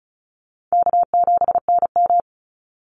Morse code, heard as a keyed tone, about 34 wpm, 700 Hz. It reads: K7NM
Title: Easy Call Signs Repeated 3x - 35wpm